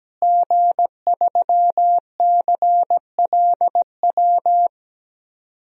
Morse code, heard as a keyed tone, about 17 wpm, 700 Hz. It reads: G3CLW
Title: DXCC-100 Call Signs - 5 Characters - Repeated 3x - 17wpm